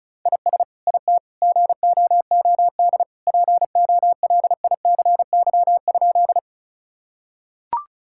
Morse code, heard as a keyed tone, about 36 wpm, 690 Hz. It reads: IS IT GOOD POLICY?
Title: Sentences from Top 1000 Words - Review Part 5 - 35wpm